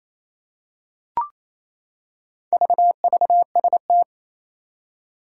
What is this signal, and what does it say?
Morse code, heard as a keyed tone, about 27 wpm, 700 Hz. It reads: VVST
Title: Four Challenging Letter-Number Combinations 28wpm